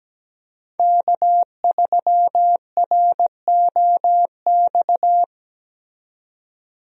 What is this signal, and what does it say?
Morse code, heard as a keyed tone, about 17 wpm, 700 Hz. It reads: K3ROX
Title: USA Call Signs - 5 Characters - 17wpm